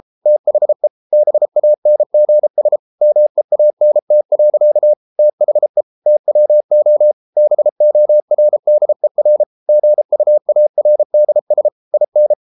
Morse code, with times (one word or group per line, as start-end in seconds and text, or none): 0.3-0.9 THE
1.1-2.8 BANGS
3.0-4.9 MEANT.
5.2-5.8 THE
6.1-7.1 TWO
7.4-9.4 BORDER
9.7-11.7 GUARDS
11.9-12.3 IN